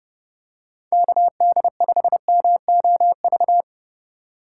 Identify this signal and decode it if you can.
Morse code, heard as a keyed tone, about 30 wpm, 700 Hz. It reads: KD5MOV
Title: USA Call Signs - 4x Character Spacing - 30wpm